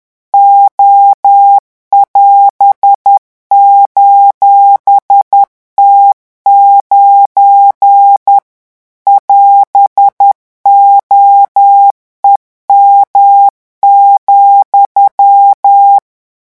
Morse code, heard as a keyed tone, about 10 wpm, 800 Hz. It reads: O<AS>:T9 <AS>OEM,
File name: morse mp3 / revision17-vitesse 10 mots